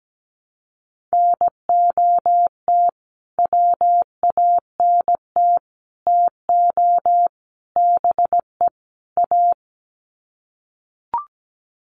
Morse code, heard as a keyed tone, about 17 wpm, 700 Hz. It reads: NOT WANT TO BE A
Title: Sets of 5 Words Encore 17wpm